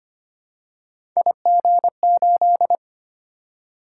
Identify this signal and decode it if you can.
Morse code, heard as a keyed tone, about 25 wpm, 700 Hz. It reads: IG8
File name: Three Letter-Number Combinations - 25wpm